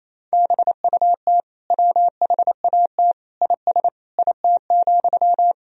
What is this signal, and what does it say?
Morse code, heard as a keyed tone, about 28 wpm, 700 Hz. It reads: BUT WHAT IS IT,
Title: Sentences from Top 300 Words and Abbreviations 28wpm